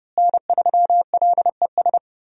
Morse code, heard as a keyed tone, about 30 wpm, 700 Hz. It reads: N3LES
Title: USA Call Signs - 5 Characters - Repeated 2x - 30wpm